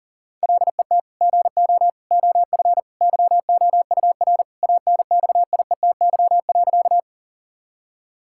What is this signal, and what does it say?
Morse code, heard as a keyed tone, about 40 wpm, 700 Hz. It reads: LET GO OF YOUR ANXIETY.
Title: Sentences from Top 1600 Words 40wpm